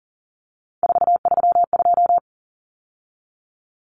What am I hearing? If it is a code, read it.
Morse code, heard as a keyed tone, about 40 wpm, 700 Hz. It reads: 432
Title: Common Cut Numbers - 3-Digit 40wpm